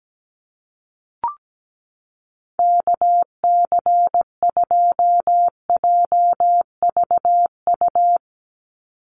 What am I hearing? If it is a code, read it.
Morse code, heard as a keyed tone, about 17 wpm, 700 Hz. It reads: KC2JVU
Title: USA Call Signs 17wpm